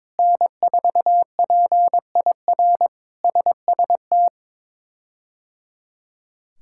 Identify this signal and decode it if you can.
Morse code, heard as a keyed tone, about 22 wpm, 700 Hz. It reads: N4PIR SST
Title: K1USN SST QSO Practice - 22wpm